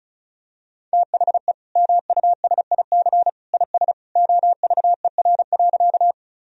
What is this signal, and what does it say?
Morse code, heard as a keyed tone, about 34 wpm, 700 Hz. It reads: THE MUSIC IS OVER.
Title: Sentences from Top 1000 Words - Review Part 1 - 35wpm